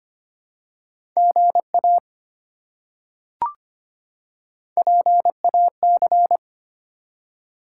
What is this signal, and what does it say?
Morse code, heard as a keyed tone, about 25 wpm, 700 Hz. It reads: GA PAC
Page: morse-code-ninja-practice-25wpm